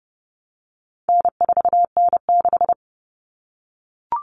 Morse code, heard as a keyed tone, about 30 wpm, 700 Hz. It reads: N4N6
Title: Common Cut Numbers - 4-Digit 30wpm